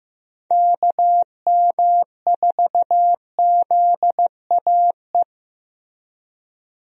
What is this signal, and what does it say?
Morse code, heard as a keyed tone, about 15 wpm, 700 Hz. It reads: KM4ZAE
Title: USA Call Signs - 6 Characters - 15wpm